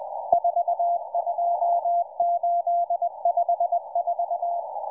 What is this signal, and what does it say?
Morse code, heard as a keyed tone, about 20 wpm, 690 Hz. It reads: 42854